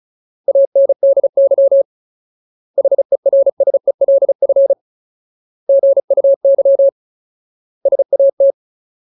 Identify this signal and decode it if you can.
Morse code, heard as a keyed tone, about 34 wpm, 550 Hz. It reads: ANDY HERSELF GUY SAT
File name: Ultimate Word List / 35WPM Ultimate